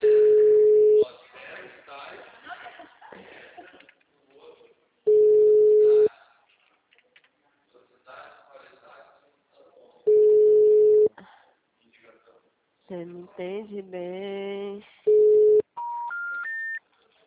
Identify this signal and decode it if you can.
Ringback tone (Europe, 425 Hz), with speech.